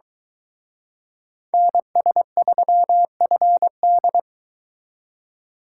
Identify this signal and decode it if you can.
Morse code, heard as a keyed tone, about 23 wpm, 700 Hz. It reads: NS3FD